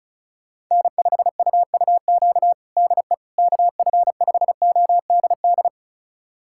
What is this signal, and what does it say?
Morse code, heard as a keyed tone, about 34 wpm, 700 Hz. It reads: N5UUQ DE KF5ODD
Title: USA Callsign DE Callsign 35wpm